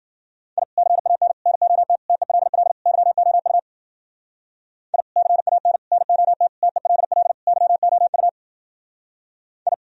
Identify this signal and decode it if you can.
Morse code, heard as a keyed tone, about 48 wpm, 700 Hz. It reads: I CAN NOT TELL YOU I CAN NOT TELL YOU I
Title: Speed-Racing - Sets of 5 Words 40wpm